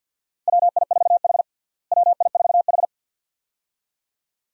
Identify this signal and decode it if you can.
Morse code, heard as a keyed tone, about 53 wpm, 700 Hz. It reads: WI4H WI4H
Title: USA Call Signs - 4 Characters - Repeated 2x - 50wpm